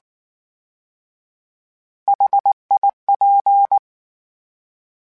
Morse code, heard as a keyed tone, about 19 wpm, 800 Hz. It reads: HIP
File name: H Prefixes / H mp3s